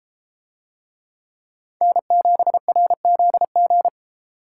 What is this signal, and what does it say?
Morse code, heard as a keyed tone, about 31 wpm, 700 Hz. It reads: N7RZG